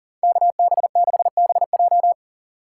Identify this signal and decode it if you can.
Morse code, heard as a keyed tone, about 37 wpm, 700 Hz. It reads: KB6BJ
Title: USA Call Signs - 5 Characters - 2x Character Spacing - 40wpm